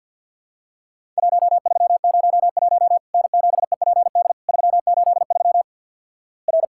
Morse code, heard as a keyed tone, about 50 wpm, 700 Hz. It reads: 1301 N7EPD 383 R
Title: Worked All Europe DX Contest - QTC Exchange with Repeats - 50wpm